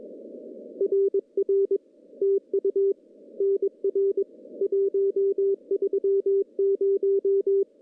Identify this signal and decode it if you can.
Morse code, heard as a keyed tone, about 22 wpm, 390 Hz. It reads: RR TU NR 130
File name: contests / CW-SS-k5tr- -1500